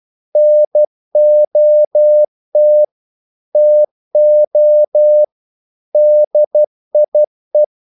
Morse code, heard as a keyed tone, about 12 wpm, 600 Hz. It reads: NOT TO DIE
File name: CW / Practice / EnglishQuotations0312